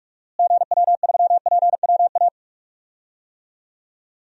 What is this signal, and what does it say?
Morse code, heard as a keyed tone, about 42 wpm, 700 Hz. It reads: GW3PWA
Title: DXCC-100 Call Signs - 6 Characters - Repeated 3x - 45wpm